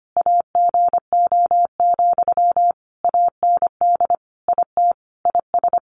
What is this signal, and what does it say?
Morse code, heard as a keyed tone, about 25 wpm, 700 Hz. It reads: AGO, AND IT IS
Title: Sentences from Top 600 Words 25wpm